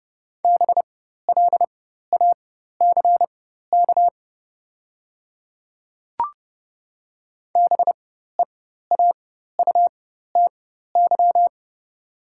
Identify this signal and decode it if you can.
Morse code, heard as a keyed tone, about 30 wpm, spaced as for 14 wpm, 700 Hz. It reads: BLACK BEAUTY
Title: 4x Character Spacing - Top 500 Words - 30wpm